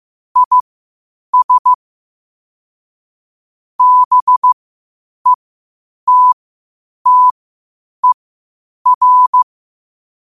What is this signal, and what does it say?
Morse code, heard as a keyed tone, about 15 wpm, spaced as for 8 wpm, 1000 Hz. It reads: IS BETTER
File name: qotdfiles / 8wpm